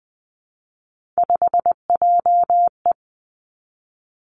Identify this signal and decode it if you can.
Morse code, heard as a keyed tone, about 20 wpm, 700 Hz. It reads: 5JE
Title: Three Challenging Letter-Number Combinations 20wpm